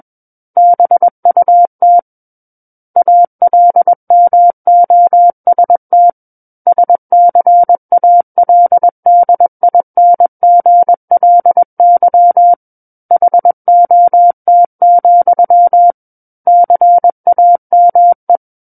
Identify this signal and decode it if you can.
Morse code, heard as a keyed tone, about 21 wpm, 700 Hz. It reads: BUT ALMOST SCALDINGLY HOT, CAME
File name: Book One / 12-Chapter 12 - 21 WPM